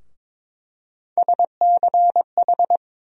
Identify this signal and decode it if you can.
Morse code, heard as a keyed tone, about 22 wpm, 700 Hz. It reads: SCH
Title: USA Call Sign Postfixes - 3 Letters 22wpm